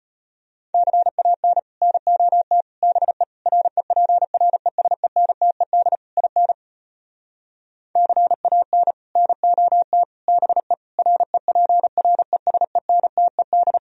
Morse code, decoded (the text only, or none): CAN NOT BE REPRESENTED IN CAN NOT BE REPRESENTED